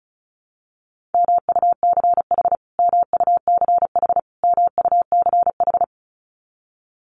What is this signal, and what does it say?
Morse code, heard as a keyed tone, about 34 wpm, 700 Hz. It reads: MUCH MUCH MUCH
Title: Top 500 Words Repeated 35wpm